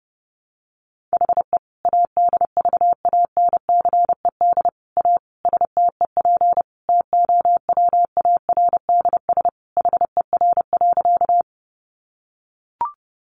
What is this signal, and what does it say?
Morse code, heard as a keyed tone, about 30 wpm, 700 Hz. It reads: HE ADVANCED A STEP TOWARDS HER.
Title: Sentences from Top 1200 Words 30wpm